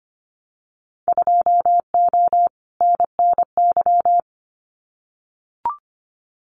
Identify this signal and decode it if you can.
Morse code, heard as a keyed tone, about 25 wpm, 700 Hz. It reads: 2O NNY
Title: Winter Field Day Exchanges 25wpm